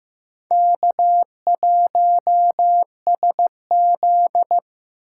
Morse code, heard as a keyed tone, about 15 wpm, 700 Hz. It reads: K1SZ
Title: USA Call Signs - 2x Character Spacing - 15wpm